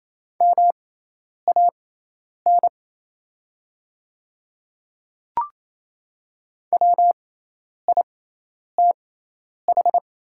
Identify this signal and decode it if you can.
Morse code, heard as a keyed tone, about 28 wpm, spaced as for 10 wpm, 700 Hz. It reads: MAN WITH
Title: 6x Character Spacing - Top 300 Words - 28wpm